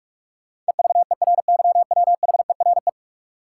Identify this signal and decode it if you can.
Morse code, heard as a keyed tone, about 46 wpm, 700 Hz. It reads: EVERYWHERE
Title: Sentences from Top 1600 Words - New Word Drill - 45wpm